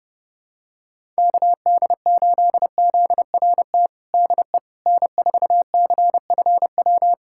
Morse code, heard as a keyed tone, about 32 wpm, 700 Hz. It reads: KD8ZRT DE N4CFW
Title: USA Callsign DE Callsign 30wpm